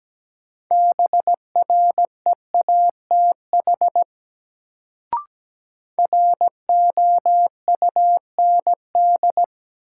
Morse code, heard as a keyed tone, about 17 wpm, 700 Hz. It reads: BREATH ROUND
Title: Sets of 5 Words Encore - New Word Drill - 17wpm